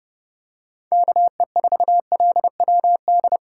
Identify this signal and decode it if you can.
Morse code, heard as a keyed tone, about 31 wpm, 700 Hz. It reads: KE4LWD
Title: USA Call Signs - 8x Character Spacing - 30wpm